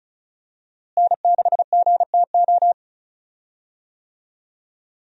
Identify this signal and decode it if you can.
Morse code, heard as a keyed tone, about 36 wpm, 700 Hz. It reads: N6GTO
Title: USA Call Signs - 5 Characters - Repeated 3x - 35wpm